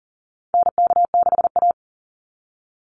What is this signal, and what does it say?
Morse code, heard as a keyed tone, about 40 wpm, 700 Hz. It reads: NK6A
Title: NA SOTA Chaser Callsigns 40wpm